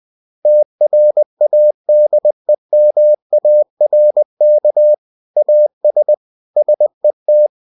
Morse code, heard as a keyed tone, about 20 wpm, 600 Hz. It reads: TRADEMARK AS SET
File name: cw-books / winnie-the-pooh / winnie-20wpm-page-0115